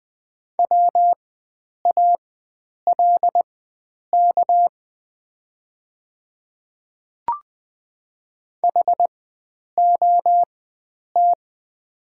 Morse code, heard as a keyed tone, about 20 wpm, spaced as for 9 wpm, 690 Hz. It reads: WALK HOT
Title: 4x Character Spacing - Top 300 Words - 20wpm